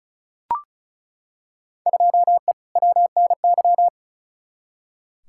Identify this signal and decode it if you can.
Morse code, heard as a keyed tone, about 36 wpm, 700 Hz. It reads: 2E WNY
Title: Field Day Exchanges Encore - 35wpm